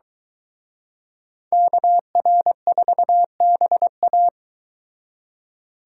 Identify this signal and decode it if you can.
Morse code, heard as a keyed tone, about 22 wpm, 700 Hz. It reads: KR4BA